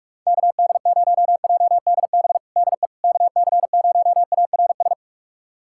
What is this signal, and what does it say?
Morse code, heard as a keyed tone, about 48 wpm, 700 Hz. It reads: KD0JDB DE KC0ARS
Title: USA Callsign DE Callsign - 6 Characters - 45wpm